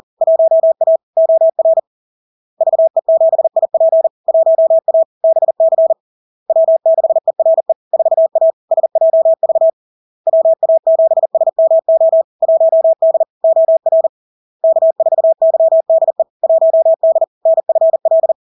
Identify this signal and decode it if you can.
Morse code, heard as a keyed tone, about 40 wpm, 650 Hz. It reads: 1A OR VE7IP 1A BC W6ERE 4A SJV WA7SMO 1D OR K4YDE 1D NFL
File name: LARK-FD-2022-Log-in-CW-40wpm-650-Hz